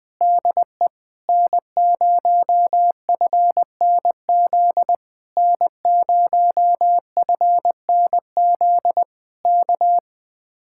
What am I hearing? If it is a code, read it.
Morse code, heard as a keyed tone, about 20 wpm, 700 Hz. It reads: DE N0FNZ N0FNZ K